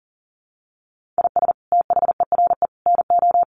Morse code, heard as a keyed tone, about 40 wpm, 700 Hz. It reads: IS THERE NO
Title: Sentences from Top 1500 Words 40wpm